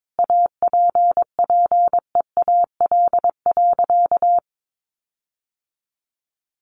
Morse code, heard as a keyed tone, about 22 wpm, 700 Hz. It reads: APPEAL.
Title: Sentences from Top 1300 Words 22wpm